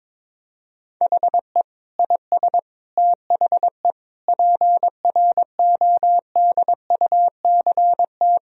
Morse code, heard as a keyed tone, about 22 wpm, 700 Hz. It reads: HE IS THE PRODUCT
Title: Sentences from Top 1000 Words - Review Part 2 - 22wpm